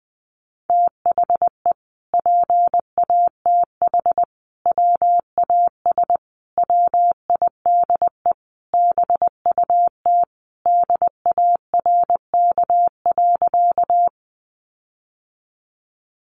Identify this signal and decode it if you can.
Morse code, heard as a keyed tone, about 20 wpm, 700 Hz. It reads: THE PATH WAS WIDE BUT DARK.
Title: Sentences from Top 1400 Words 20wpm